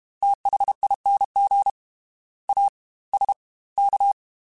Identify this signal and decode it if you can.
Morse code, heard as a keyed tone, about 32 wpm, 800 Hz. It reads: THING A S K
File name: Special INGs / Special INGs-mp3s / Fast ing @30